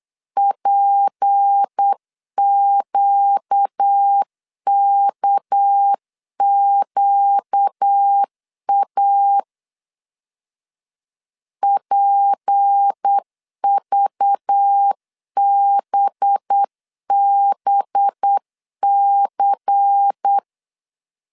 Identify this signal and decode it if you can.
Morse code, heard as a keyed tone, about 8 wpm, 800 Hz. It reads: PQKQA PVBBC
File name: CW / DET / 1 6 1500C S C 8wpm 5219